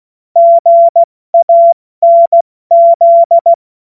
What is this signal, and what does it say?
Morse code, heard as a keyed tone, about 16 wpm, spaced as for 14 wpm, 680 Hz. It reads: GANZ
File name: Morsekurs / Witze / Witz 17.Oktober 2023 70 F80